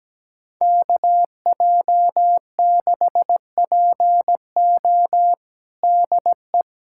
Morse code, heard as a keyed tone, about 17 wpm, 700 Hz. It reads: KJ6PO DE
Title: USA Callsign DE Callsign - 5 Characters - 17wpm